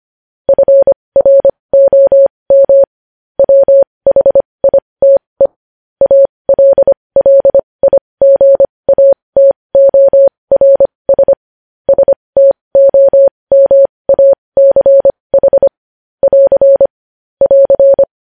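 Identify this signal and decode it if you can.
Morse code, heard as a keyed tone, about 25 wpm, spaced as for 20 wpm, 550 Hz. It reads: FROM WHITE ALLIGATORS STOMACH <AR> <AR>
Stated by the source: Morse news headlines sent at 20.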